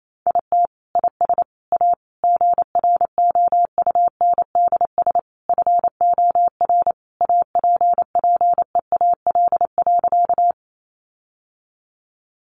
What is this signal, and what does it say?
Morse code, heard as a keyed tone, about 28 wpm, 700 Hz. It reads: IT IS A GROUNDS FOR APPEAL.
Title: Sentences from Top 1300 Words 28wpm